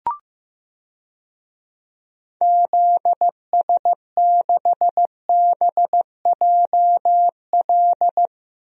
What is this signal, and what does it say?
Morse code, heard as a keyed tone, about 15 wpm, 700 Hz. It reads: ZS6BJL
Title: DXCC-100 Call Signs - 6 Characters - 15wpm